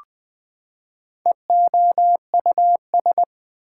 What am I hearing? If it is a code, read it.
Morse code, heard as a keyed tone, about 20 wpm, 700 Hz. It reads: EOUS
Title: Prefixes and Suffixes 20wpm